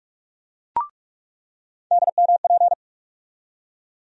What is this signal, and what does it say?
Morse code, heard as a keyed tone, about 46 wpm, 700 Hz. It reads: DMP
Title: USA Call Sign Postfixes - 3 Letters 45wpm